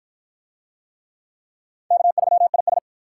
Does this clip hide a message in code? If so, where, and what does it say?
Yes: Morse code from 1.9 to 2.8 s, heard as a keyed tone, about 51 wpm, 700 Hz. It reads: K3IS